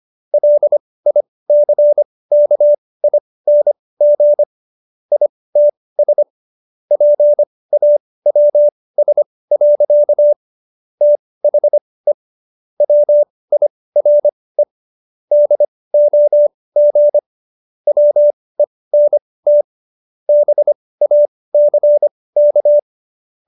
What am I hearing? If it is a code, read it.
Morse code, heard as a keyed tone, about 25 wpm, spaced as for 18 wpm, 590 Hz. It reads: LICKING ITS PAWS. THE WIRE DOG WENT BACK